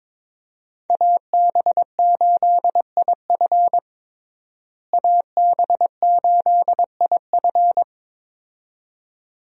Morse code, heard as a keyed tone, about 22 wpm, 700 Hz. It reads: AB8IF AB8IF
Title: USA Call Signs - 5 Characters - Repeated 2x - 22wpm